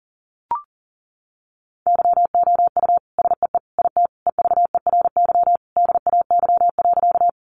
Morse code, heard as a keyed tone, about 39 wpm, 700 Hz. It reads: YOU SEE IT EVERY DAY.
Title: Sentences from Top 200 Words 40wpm